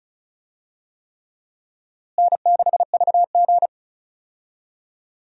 Morse code, heard as a keyed tone, about 36 wpm, 700 Hz. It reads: N6VG
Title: USA Call Signs - 4 Characters - Repeated 3x - 35wpm